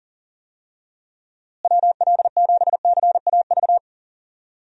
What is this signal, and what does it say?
Morse code, heard as a keyed tone, about 38 wpm, 700 Hz. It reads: WL7CAV
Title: USA Call Signs - 6 Characters - Repeated 3x - 40wpm